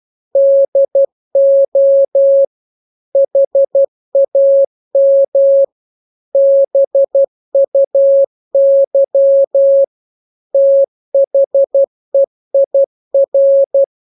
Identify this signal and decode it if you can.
Morse code, heard as a keyed tone, about 12 wpm, 550 Hz. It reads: DO HAM BUY THEIR
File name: jokes / joke 19 12 0000